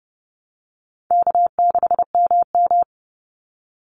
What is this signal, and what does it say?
Morse code, heard as a keyed tone, about 30 wpm, 700 Hz. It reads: K6MM
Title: Easy Call Signs Repeated 3x - 30wpm